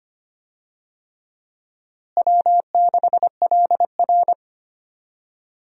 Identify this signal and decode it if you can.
Morse code, heard as a keyed tone, about 25 wpm, 700 Hz. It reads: W6LR
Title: USA Call Signs - 4 Characters - Repeated 3x - 25wpm